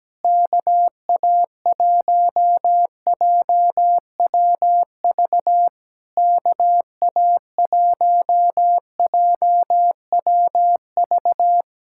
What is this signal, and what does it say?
Morse code, heard as a keyed tone, about 17 wpm, 700 Hz. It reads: KA1JWV KA1JWV
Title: Speed-Racing Common QSO Phrases - 17wpm